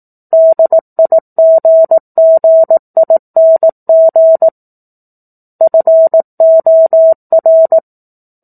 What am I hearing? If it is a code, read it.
Morse code, heard as a keyed tone, about 18 wpm, 650 Hz. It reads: DIGGING FOR
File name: 04-Chapter-04 18 WPM